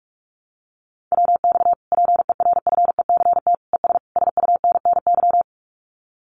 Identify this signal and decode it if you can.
Morse code, heard as a keyed tone, about 46 wpm, 700 Hz. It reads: WX PERFECT ES SUNNY
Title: Common QSO Phrases Encore 45wpm